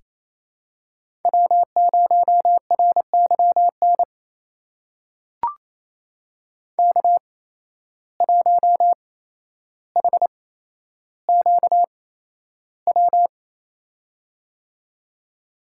Morse code, heard as a keyed tone, about 27 wpm, spaced as for 12 wpm, 700 Hz. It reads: W0RYN K1HQW
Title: USA Call Signs - 5 Characters - 8x Character Spacing - 28wpm